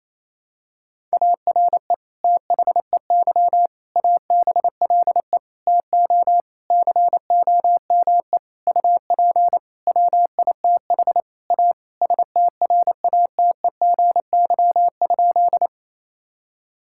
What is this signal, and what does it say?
Morse code, heard as a keyed tone, about 27 wpm, 700 Hz. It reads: ARE THEY ABLE TO COME UP WITH A STRATEGY?
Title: Sentences from Top 1000 Words - Review Part 5 - 28wpm